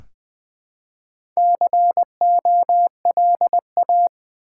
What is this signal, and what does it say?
Morse code, heard as a keyed tone, about 20 wpm, 700 Hz. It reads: COLA
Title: Easy Words with Hints - 20wpm